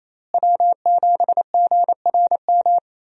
Morse code, heard as a keyed tone, about 27 wpm, 700 Hz. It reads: W7GRM
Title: USA Call Signs - 5 Characters - Repeated 2x - 28wpm